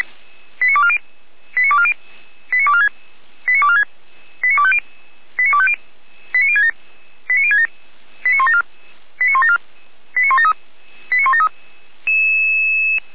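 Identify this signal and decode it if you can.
Five-tone selective call (ZVEI), heard as quick runs of five tones, tones of about 70 ms. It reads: 89240 89240 89246 89246 89250 89250 89067 89067 89174 89174 89173 89173
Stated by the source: ZVEI five-tone call received on an SDR, from a fire brigade server.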